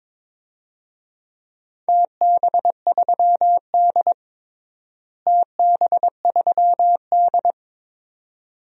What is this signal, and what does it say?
Morse code, heard as a keyed tone, about 22 wpm, 700 Hz. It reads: TB3D TB3D
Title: DXCC-100 Call Signs - 4 Characters - Repeated 2x - 22wpm